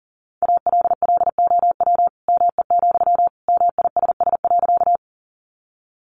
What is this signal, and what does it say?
Morse code, heard as a keyed tone, about 40 wpm, 700 Hz. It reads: ALLOW ME, MISS.
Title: Sentences from Top 700 Words 40wpm